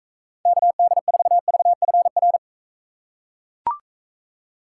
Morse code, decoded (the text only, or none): KD4VFR